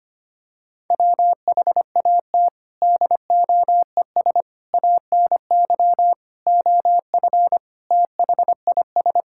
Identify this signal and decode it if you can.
Morse code, heard as a keyed tone, about 25 wpm, 700 Hz. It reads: WHAT DOES ANY OF THIS
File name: Sets of 9 Words - 25wpm